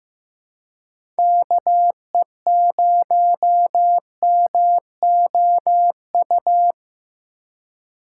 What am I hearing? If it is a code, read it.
Morse code, heard as a keyed tone, about 15 wpm, 700 Hz. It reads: KE0MOU
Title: USA Call Signs - 6 Characters - 3x Character Spacing - 15wpm